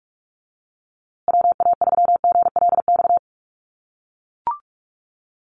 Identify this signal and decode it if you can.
Morse code, heard as a keyed tone, about 44 wpm, 690 Hz. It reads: WA3GLX
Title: USA Call Signs Repeated 2x 45wpm